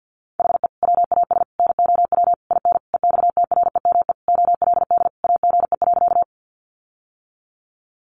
Morse code, heard as a keyed tone, about 50 wpm, 700 Hz. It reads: HE WAS NOW IN EXTREME OLD AGE.
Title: Sentences from Top 1300 Words 50wpm